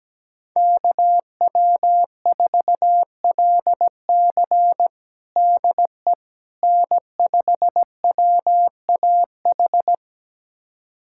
Morse code, heard as a keyed tone, about 17 wpm, 700 Hz. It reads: KW4LC DE N5WAH
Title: USA Callsign DE Callsign - 5 Characters - 17wpm